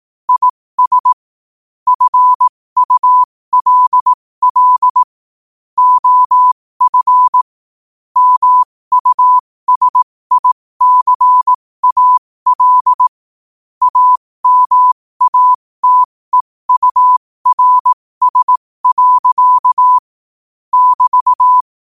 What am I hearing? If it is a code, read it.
Morse code, heard as a keyed tone, about 18 wpm, spaced as for 15 wpm, 1000 Hz. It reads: IS FULL OF MUSICAL AMATEURS. <BT>
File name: 15wpm